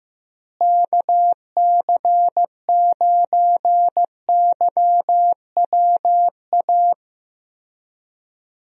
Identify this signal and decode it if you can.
Morse code, heard as a keyed tone, about 15 wpm, 700 Hz. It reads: KC9YWA
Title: USA Call Signs - 6 Characters - Repeated 2x - 15wpm